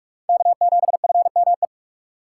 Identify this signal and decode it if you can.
Morse code, heard as a keyed tone, about 58 wpm, 700 Hz. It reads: K7FME